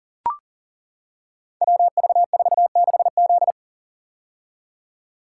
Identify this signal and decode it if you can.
Morse code, heard as a keyed tone, about 42 wpm, 700 Hz. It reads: WV46Z
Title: Code Groups - 5 Letter:Numbers - 40wpm